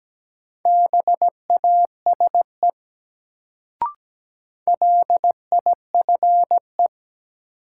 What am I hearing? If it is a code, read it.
Morse code, heard as a keyed tone, about 17 wpm, 700 Hz. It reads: BASE LIFE
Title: Sets of 5 Words - New Word Drill - 17wpm